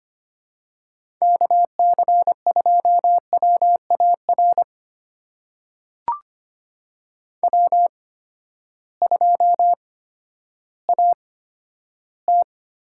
Morse code, heard as a keyed tone, about 25 wpm, spaced as for 12 wpm, 700 Hz. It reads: KC2WAR W2AT
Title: USA Call Signs - 8x Character Spacing - 25wpm